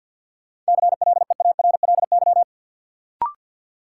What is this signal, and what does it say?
Morse code, heard as a keyed tone, about 53 wpm, 700 Hz. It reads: CLEARLY
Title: Sentences from Top 1300 Words - New Word Drill - 50wpm